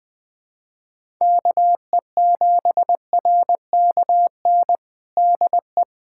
Morse code, heard as a keyed tone, about 20 wpm, 700 Hz. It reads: KE7RKN DE
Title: USA Callsign DE Callsign 20wpm